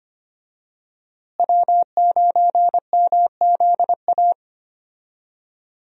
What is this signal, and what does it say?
Morse code, heard as a keyed tone, about 25 wpm, 700 Hz. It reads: W9MZA
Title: USA Call Signs - 5 Characters - Repeated 2x - 25wpm